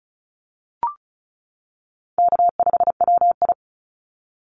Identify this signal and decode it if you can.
Morse code, heard as a keyed tone, about 34 wpm, 700 Hz. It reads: K5WI
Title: NA SOTA Chaser Callsigns 35wpm